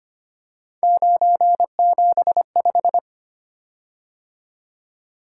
Morse code, heard as a keyed tone, about 25 wpm, 700 Hz. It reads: 975
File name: Three Letter-Number Combinations - 25wpm